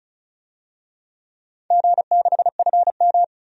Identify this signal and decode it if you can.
Morse code, heard as a keyed tone, about 36 wpm, 700 Hz. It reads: G6FM
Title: DXCC-100 Call Signs - 4 Characters - 35wpm